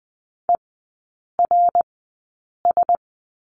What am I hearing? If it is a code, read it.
Morse code, heard as a keyed tone, about 20 wpm, spaced as for 8 wpm, 700 Hz. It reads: ERS
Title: 067 Lesson 12e - Letter d - Three Characters 20wpm